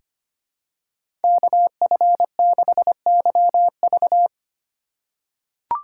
Morse code, heard as a keyed tone, about 25 wpm, 700 Hz. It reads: KF6YV
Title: USA Call Signs - 2x Character Spacing - 25wpm